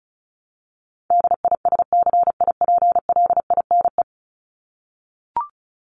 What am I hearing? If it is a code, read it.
Morse code, heard as a keyed tone, about 34 wpm, 700 Hz. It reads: DISCIPLINE
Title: Top 751-1000 Words 35wpm